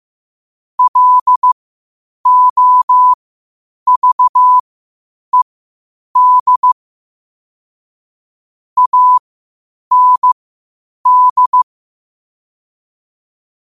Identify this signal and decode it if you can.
Morse code, heard as a keyed tone, about 15 wpm, spaced as for 8 wpm, 1000 Hz. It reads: LOVED AND
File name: qotdfiles / 8wpm